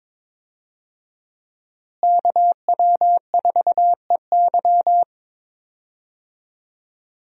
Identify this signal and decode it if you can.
Morse code, heard as a keyed tone, about 22 wpm, 700 Hz. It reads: KW4EY
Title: USA Call Signs - 5 Characters - Repeated 3x - 22wpm